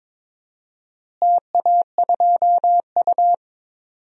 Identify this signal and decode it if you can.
Morse code, heard as a keyed tone, about 22 wpm, 700 Hz. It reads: TA2U
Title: DXCC-100 Call Signs - 4 Characters - Repeated 3x - 22wpm